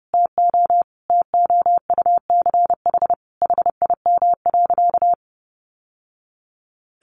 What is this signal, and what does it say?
Morse code, heard as a keyed tone, about 30 wpm, 700 Hz. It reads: TO TOUCH HIM.
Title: Sentences from Top 1200 Words 30wpm